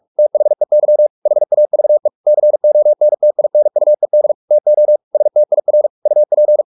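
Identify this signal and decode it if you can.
Morse code, heard as a keyed tone, about 44 wpm, 600 Hz. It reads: THEY HAVE CONTINUED TO STIR UP
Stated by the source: Morse practice at 45 wpm, the novel Never by Ken Follett in English, from CW med Gnister.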